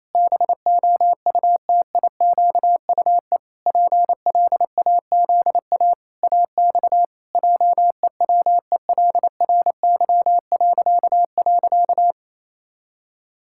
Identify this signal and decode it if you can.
Morse code, heard as a keyed tone, about 27 wpm, 700 Hz. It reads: BOUTIQUE PLAZA AX JEWELRY..
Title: Pangram Challenge v2 - 28wpm